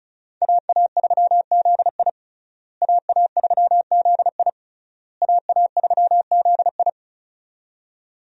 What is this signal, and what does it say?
Morse code, heard as a keyed tone, about 34 wpm, 700 Hz. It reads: AA3ZI AA3ZI AA3ZI
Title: USA Call Signs - 5 Characters - Repeated 3x - 35wpm